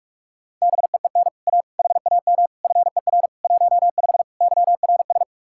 Morse code, heard as a keyed tone, about 50 wpm, 700 Hz. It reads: BEEN A HAM FER 15 YRS